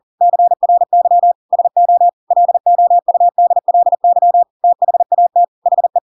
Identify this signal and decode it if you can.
Morse code, heard as a keyed tone, about 40 wpm, 700 Hz. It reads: CRY SO LOUDLY THAT HE
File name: Pinocchio / Pinocchio ch7 40 40 ik1hge